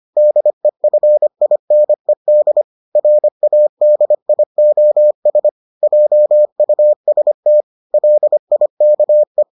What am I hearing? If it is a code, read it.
Morse code, heard as a keyed tone, about 25 wpm, 600 Hz. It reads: DEFINED RADIOS JUST LIKE